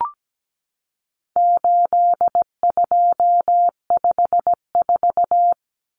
Morse code, heard as a keyed tone, about 17 wpm, 700 Hz. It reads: 8254
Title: Common Cut Numbers - 4-Digit 17wpm